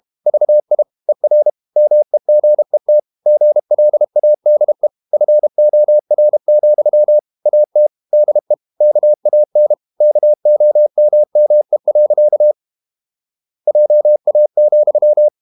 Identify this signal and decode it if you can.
Morse code, heard as a keyed tone, about 32 wpm, 600 Hz. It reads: VI ER MEGET GLADE FOR, AT DE KAN KOMME. JA,
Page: Ferie 32 wpm